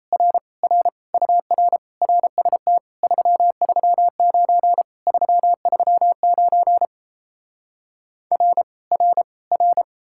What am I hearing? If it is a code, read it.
Morse code, heard as a keyed tone, about 38 wpm, 700 Hz. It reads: R R UR RST 339 339 R R R